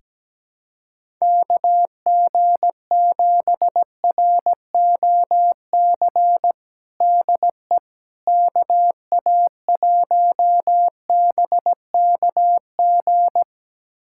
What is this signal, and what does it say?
Morse code, heard as a keyed tone, about 17 wpm, 700 Hz. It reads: KG7ROC DE KA1BKG
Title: USA Callsign DE Callsign - 6 Characters - 17wpm